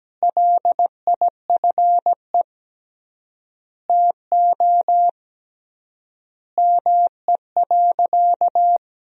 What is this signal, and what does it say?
Morse code, heard as a keyed tone, about 17 wpm, 700 Hz. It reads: LIFE TO ME.
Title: 3x Word-Spacing - Sentences from Top 500 Words - 17wpm